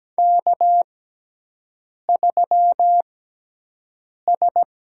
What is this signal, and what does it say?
Morse code, heard as a keyed tone, about 17 wpm, spaced as for 6 wpm, 700 Hz. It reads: K3S
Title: USA Call Signs - 4 Characters - 6x Character Spacing - 17wpm